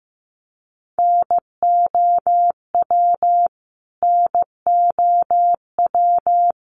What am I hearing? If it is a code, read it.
Morse code, heard as a keyed tone, about 15 wpm, 700 Hz. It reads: NOW NOW
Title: Top 200 Words Repeated 15wpm